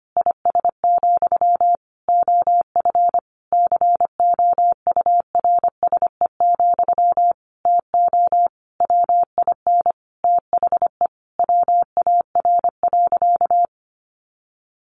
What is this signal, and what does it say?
Morse code, heard as a keyed tone, about 25 wpm, 700 Hz. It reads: IS, OF COURSE, TO WIN THE WAR.
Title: Sentences from Top 1500 Words 25wpm